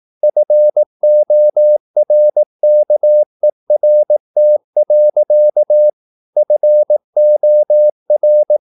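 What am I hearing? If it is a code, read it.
Morse code, heard as a keyed tone, about 18 wpm, 600 Hz. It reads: FORKERT. FOR
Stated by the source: Slow (QRS) Morse practice at 18 wpm, CW med Gnister.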